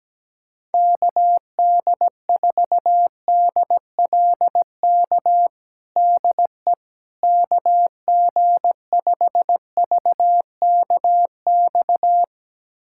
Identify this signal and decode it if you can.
Morse code, heard as a keyed tone, about 17 wpm, 700 Hz. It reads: KD4DLK DE KG5VKX
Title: USA Callsign DE Callsign - 6 Characters - 17wpm